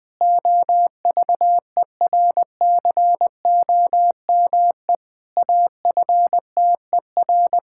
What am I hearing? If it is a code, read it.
Morse code, heard as a keyed tone, about 20 wpm, 700 Hz. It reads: OVERCOME AFTER
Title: Pangram Challenge v2 - 20wpm